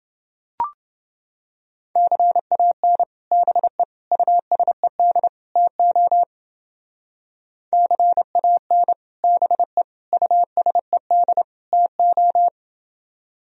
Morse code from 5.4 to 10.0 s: TO CAN BE

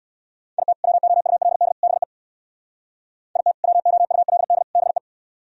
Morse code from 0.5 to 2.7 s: IT WOULD BE

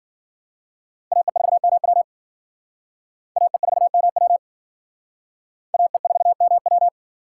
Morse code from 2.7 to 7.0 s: AE4MW AE4MW

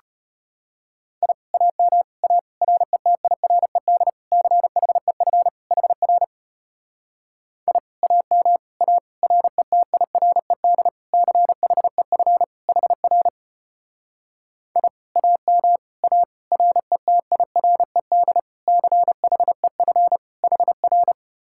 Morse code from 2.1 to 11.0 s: A RETIRED CHEF HR I AM A RETIRED

